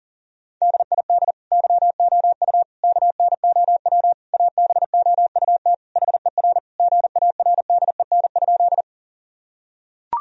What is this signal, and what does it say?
Morse code, heard as a keyed tone, about 40 wpm, 700 Hz. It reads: DID YOU KNOW ABOUT HER GARDEN?
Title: Sentences from Top 1000 Words - Review Part 4 - 40wpm